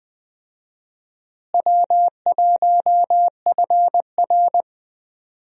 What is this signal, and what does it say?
Morse code, heard as a keyed tone, about 20 wpm, 700 Hz. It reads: W1FR
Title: USA Call Signs - 4 Characters - Repeated 3x - 20wpm